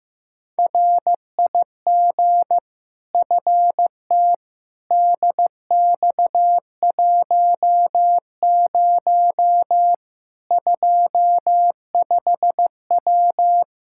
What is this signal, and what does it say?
Morse code, heard as a keyed tone, about 15 wpm, 700 Hz. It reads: RIG FT DX10 25W
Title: Practice QSO - Exchange 2 - Info - 15wpm